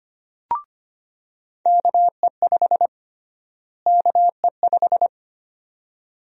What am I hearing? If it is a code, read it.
Morse code, heard as a keyed tone, about 25 wpm, 700 Hz. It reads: KE5 KE5
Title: USA Call Sign Prefixes - Repeated 2x - 25wpm